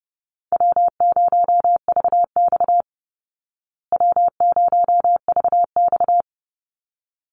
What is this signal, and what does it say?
Morse code, heard as a keyed tone, about 30 wpm, 700 Hz. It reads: W0VX W0VX
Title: Easy Call Signs Repeated 2x - 30wpm